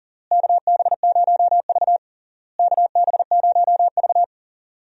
Morse code, heard as a keyed tone, about 40 wpm, 700 Hz. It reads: KB0V KB0V
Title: USA Call Signs - 4 Characters - Repeated 3x - 40wpm